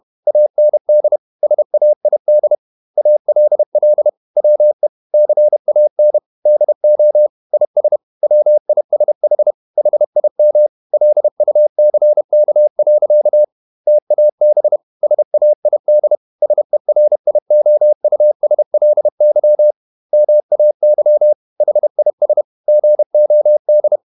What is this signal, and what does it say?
Morse code, heard as a keyed tone, about 31 wpm, 600 Hz. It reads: AND SAID ALL WE CAN DO IS WISH HIM LUCK. TAB SAID SERIOUSLY MAY HIS GOD
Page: Never 31wpm | CW med Gnister